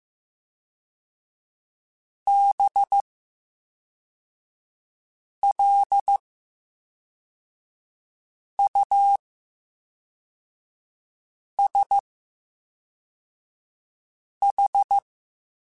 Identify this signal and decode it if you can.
Morse code, heard as a keyed tone, about 15 wpm, 800 Hz. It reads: B L U S H
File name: Special INGs / Special INGs-mp3s / VerySlowToFaster Ing @10